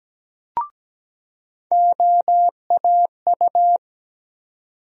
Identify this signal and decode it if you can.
Morse code, heard as a keyed tone, about 17 wpm, 700 Hz. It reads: OAU
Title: USA Call Sign Postfixes - 3 Letters 17wpm